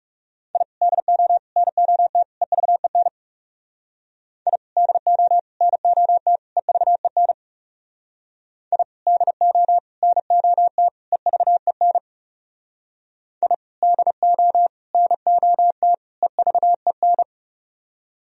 Morse code, heard as a keyed tone, about 37 wpm, 700 Hz. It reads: I DO NOT EVEN I DO NOT EVEN I DO NOT EVEN I DO NOT EVEN
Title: Speed-Racing - Sets of 4 Words 30wpm